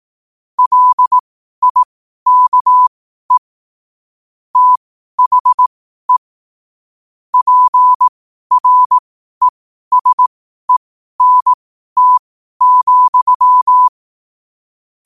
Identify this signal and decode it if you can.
Morse code, heard as a keyed tone, about 18 wpm, spaced as for 12 wpm, 1000 Hz. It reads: LIKE THE PRESENT,
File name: qotdfiles / 12wpm